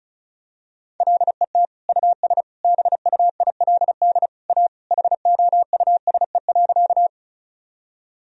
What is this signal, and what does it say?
Morse code, heard as a keyed tone, about 34 wpm, 700 Hz. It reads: LET US BUILD A HOUSE.
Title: Sentences from Top 1000 Words - Review Part 1 - 35wpm